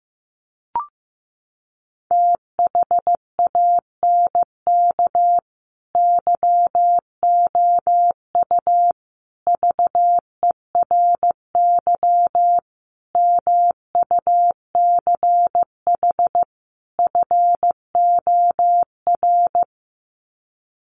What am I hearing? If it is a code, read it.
Morse code, heard as a keyed tone, about 15 wpm, 700 Hz. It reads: THANK YOU VERY MUCH FOR
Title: Sets of 5 Words 15wpm